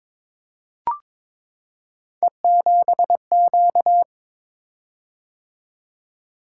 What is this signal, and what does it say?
Morse code, heard as a keyed tone, about 22 wpm, 700 Hz. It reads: E7Q
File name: practice / 22wpm / Three Letter-Number Combinations - 22wpm